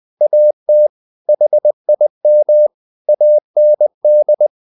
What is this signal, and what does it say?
Morse code, heard as a keyed tone, about 20 wpm, 600 Hz. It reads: AT HIM AND
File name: the old man and the sea part 2 20wpm